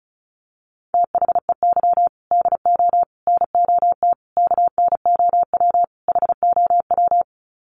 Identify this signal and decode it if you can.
Morse code, heard as a keyed tone, about 34 wpm, 700 Hz. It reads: THEY DO NOT KNOW HOW
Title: Sets of 5 Words Encore2 35wpm